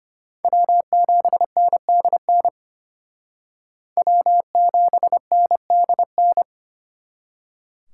Morse code, heard as a keyed tone, about 32 wpm, 700 Hz. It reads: W7NDN W7NDN